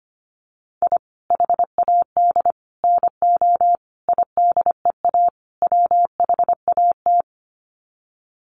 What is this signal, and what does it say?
Morse code, heard as a keyed tone, about 25 wpm, 700 Hz. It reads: I HAD NO IDEA WHAT
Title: Sets of 5 Words Encore 25wpm